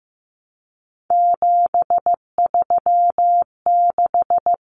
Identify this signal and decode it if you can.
Morse code, heard as a keyed tone, about 15 wpm, 700 Hz. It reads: 736
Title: Common Cut Numbers - 3-Digit 15wpm